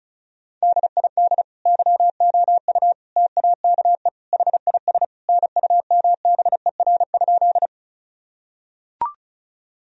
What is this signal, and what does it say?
Morse code, heard as a keyed tone, about 36 wpm, 700 Hz. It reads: DID YOU TAKE HIS NUMBER?
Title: Sentences from Top 1000 Words - Review Part 1 - 35wpm